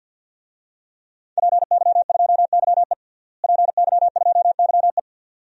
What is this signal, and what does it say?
Morse code, heard as a keyed tone, about 52 wpm, 700 Hz. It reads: PY2YE PY2YE
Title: DXCC-100 Call Signs - 5 Characters - Repeated 2x - 50wpm